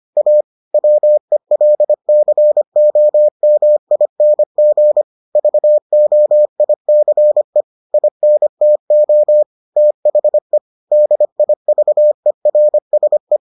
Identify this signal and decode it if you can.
Morse code, heard as a keyed tone, about 25 wpm, 600 Hz. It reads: A WELCOMING VOICE INTO THE DIVERSE